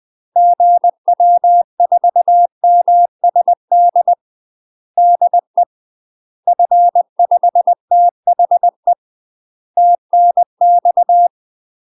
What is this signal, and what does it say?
Morse code, heard as a keyed tone, about 20 wpm, 700 Hz. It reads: GW4MSD DE F5THE TNX
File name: UK MASTER QSOs TRAINING / 20WPM / UK MASTER QSOs 0001